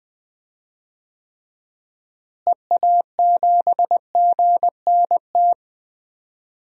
Morse code, heard as a keyed tone, about 20 wpm, 700 Hz. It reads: EA7GNT
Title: DXCC-100 Call Signs - 6 Characters - Repeated 3x - 20wpm